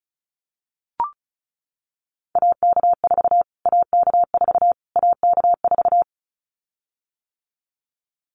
Morse code, heard as a keyed tone, about 34 wpm, 700 Hz. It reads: AK4 AK4 AK4
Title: Callsign Prefixes for Top 25 DXCC - Repeated 3x - 35wpm